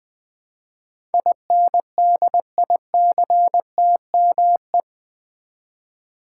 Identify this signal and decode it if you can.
Morse code, heard as a keyed tone, about 20 wpm, 700 Hz. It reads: INDICTME
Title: Build A Word - 4 Letters At A Time - 10-Letter Words - 20wpm